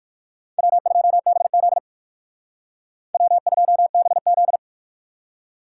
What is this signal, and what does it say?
Morse code, heard as a keyed tone, about 64 wpm, spaced as for 56 wpm, 700 Hz. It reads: W2BZ W2BZ